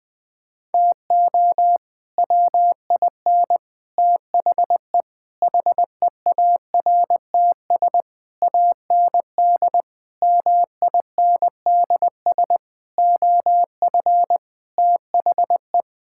Morse code, heard as a keyed tone, about 20 wpm, 700 Hz. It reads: TO WIN THE HEARTS AND MINDS OF THE
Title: Sets of 8 Words - 20wpm